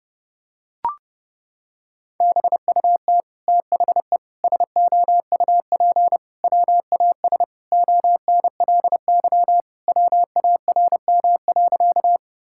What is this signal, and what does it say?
Morse code, heard as a keyed tone, about 30 wpm, 700 Hz. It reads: BUT THE SOUP WAS ONLY WARM.
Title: Sentences from Top 1000 Words - Review Part 6 - 30wpm